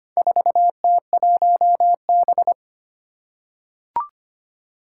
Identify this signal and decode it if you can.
Morse code, heard as a keyed tone, about 25 wpm, 700 Hz. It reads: 4T1B
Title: Four Challenging Letter-Number Combinations 25wpm